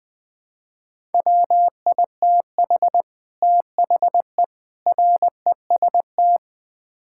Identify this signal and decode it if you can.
Morse code, heard as a keyed tone, about 20 wpm, 700 Hz. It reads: WITH THE REST
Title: Build A Phrase - 5-Word Phrases - 20wpm